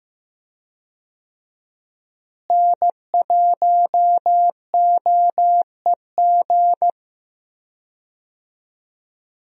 Morse code, heard as a keyed tone, about 15 wpm, 700 Hz. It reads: N1OEG
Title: USA Call Signs - 5 Characters - Repeated 2x - 15wpm